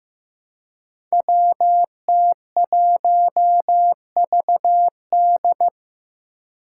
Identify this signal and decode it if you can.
Morse code, heard as a keyed tone, about 15 wpm, 700 Hz. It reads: WT1VD
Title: Five Challenging Letter-Number Combinations 15wpm